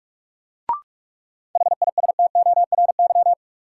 Morse code, heard as a keyed tone, about 46 wpm, 700 Hz. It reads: HISTORY
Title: Sentences from Top 600 Words - New Word Drill - 45wpm